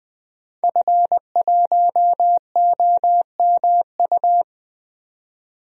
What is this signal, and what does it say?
Morse code, heard as a keyed tone, about 20 wpm, 700 Hz. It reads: F1OMU
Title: DXCC-100 Call Signs - 5 Characters - 20wpm